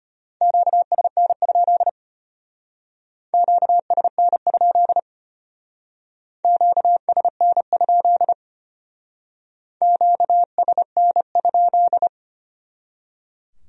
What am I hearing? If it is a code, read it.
Morse code, heard as a keyed tone, about 31 wpm, 700 Hz. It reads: QSN? QSN? QSN? QSN?
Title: Speed-Racing Q-Codes - 25wpm